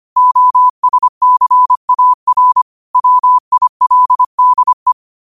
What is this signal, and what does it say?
Morse code, heard as a keyed tone, about 25 wpm, 1000 Hz. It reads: OSCAR WILDE
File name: qotdfiles / 25wpm